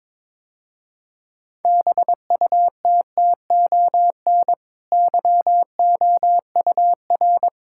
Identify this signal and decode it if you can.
Morse code, heard as a keyed tone, about 21 wpm, 700 Hz. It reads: BUTTON YOUR
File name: practice / 22wpm / Easy Sentences with Hints - 22wpm